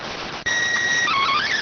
SSTV (slow-scan television), PD 290.